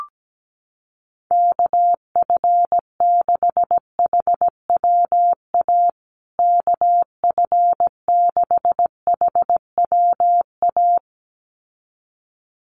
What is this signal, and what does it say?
Morse code, heard as a keyed tone, about 17 wpm, 700 Hz. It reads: KF6HWA KF6HWA
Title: USA Call Signs Repeated 2x 17wpm